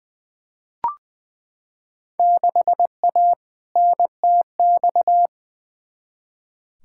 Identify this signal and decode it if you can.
Morse code, heard as a keyed tone, about 20 wpm, 700 Hz. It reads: 6A NTX
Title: Field Day Exchanges - 20wpm